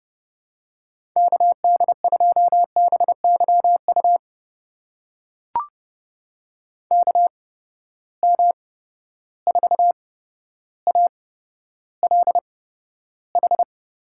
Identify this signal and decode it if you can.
Morse code, heard as a keyed tone, about 31 wpm, spaced as for 13 wpm, 700 Hz. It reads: KD2BYU KM4ALH
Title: USA Call Signs - 6 Characters - 8x Character Spacing - 30wpm